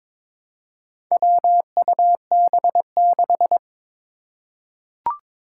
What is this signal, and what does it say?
Morse code, heard as a keyed tone, about 22 wpm, 700 Hz. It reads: WUB6
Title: Four Challenging Letter-Number Combinations 22wpm